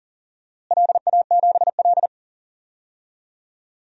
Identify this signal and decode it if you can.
Morse code, heard as a keyed tone, about 37 wpm, 700 Hz. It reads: LA7L
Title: DXCC-100 Call Signs - 4 Characters - Repeated 2x - 40wpm